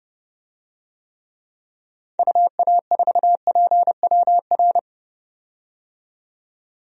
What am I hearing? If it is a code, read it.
Morse code, heard as a keyed tone, about 31 wpm, 700 Hz. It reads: UA4PWR
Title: DXCC-100 Call Signs - 6 Characters - 30wpm